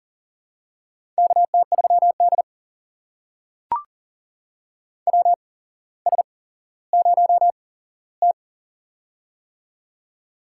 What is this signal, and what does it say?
Morse code, heard as a keyed tone, about 40 wpm, spaced as for 17 wpm, 700 Hz. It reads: KT3D WS0T
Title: USA Call Signs - 4 Characters - 8x Character Spacing - 40wpm